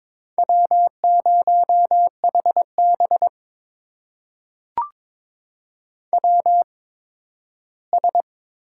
Morse code, heard as a keyed tone, about 22 wpm, spaced as for 13 wpm, 700 Hz. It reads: W0HB WS
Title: USA Call Signs - 8x Character Spacing - 22wpm